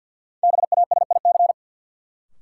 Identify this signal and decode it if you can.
Morse code, heard as a keyed tone, about 53 wpm, 700 Hz. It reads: BASIC
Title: Sentences from Top 1000 Words - Review Part 3 - New Word Drill - 50wpm